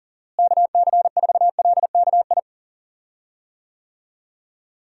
Morse code, heard as a keyed tone, about 37 wpm, 700 Hz. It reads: KC4LKI
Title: USA Call Signs - 6 Characters - Repeated 3x - 40wpm